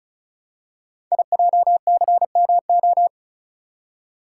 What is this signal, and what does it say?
Morse code, heard as a keyed tone, about 36 wpm, 690 Hz. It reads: IJCMO
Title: Code Groups - 5 Letters - 35wpm